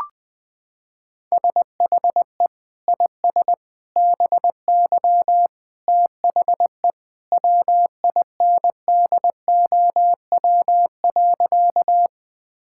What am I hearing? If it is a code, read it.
Morse code, heard as a keyed tone, about 21 wpm, 700 Hz. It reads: SHE IS BY THE WINDOW.
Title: Sentences from Top 1600 Words 20wpm